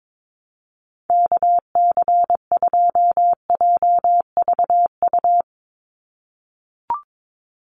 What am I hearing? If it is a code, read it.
Morse code, heard as a keyed tone, about 22 wpm, 700 Hz. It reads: KC2JVU
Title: USA Call Signs 22wpm